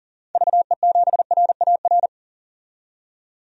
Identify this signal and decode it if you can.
Morse code, heard as a keyed tone, about 37 wpm, 700 Hz. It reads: VE7RAR
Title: DXCC-100 Call Signs - 6 Characters - 40wpm